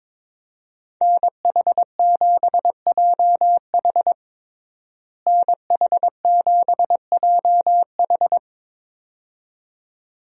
Morse code, heard as a keyed tone, about 22 wpm, 700 Hz. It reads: NH7JH NH7JH
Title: USA Call Signs - 5 Characters - Repeated 2x - 22wpm